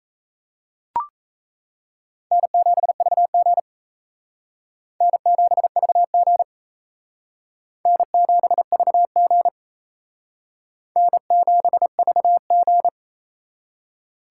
Morse code, heard as a keyed tone, about 34 wpm, 700 Hz. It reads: N7VG N7VG N7VG N7VG
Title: Speed Racing - USA 1x2 Call Signs 28wpm